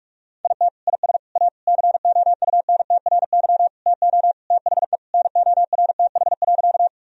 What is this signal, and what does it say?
Morse code, heard as a keyed tone, about 42 wpm, 700 Hz. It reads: IT IS A COUNTRY TO THE NORTH.
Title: Sentences from Top 1000 Words - Review Part 1 - 45wpm